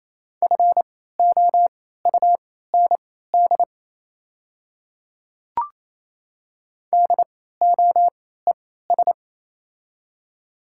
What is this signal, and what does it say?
Morse code, heard as a keyed tone, about 28 wpm, spaced as for 16 wpm, 700 Hz. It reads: FOUND DOES
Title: 3x Character Spacing - Top 300 Words - 28wpm